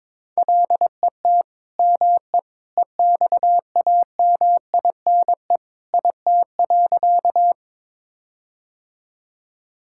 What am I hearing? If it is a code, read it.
Morse code, heard as a keyed tone, about 22 wpm, 700 Hz. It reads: LET ME EXAMINE IT.
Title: Sentences from Top 1600 Words 22wpm